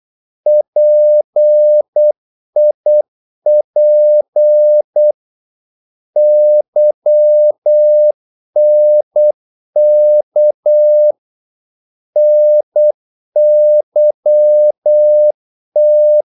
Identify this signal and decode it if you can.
Morse code, heard as a keyed tone, about 8 wpm, 600 Hz. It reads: PIP YNK NYT
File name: Korte ord DK 8wpm